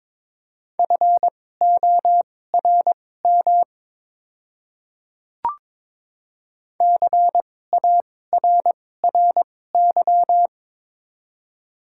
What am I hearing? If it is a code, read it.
Morse code, heard as a keyed tone, about 22 wpm, spaced as for 16 wpm, 690 Hz. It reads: FORM CARRY
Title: 2x Character Spacing - Top 500 Words - 22wpm